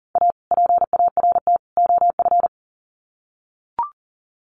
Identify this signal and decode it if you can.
Morse code, heard as a keyed tone, about 40 wpm, 700 Hz. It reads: A PART OF
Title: Sets of 3 Words 40wpm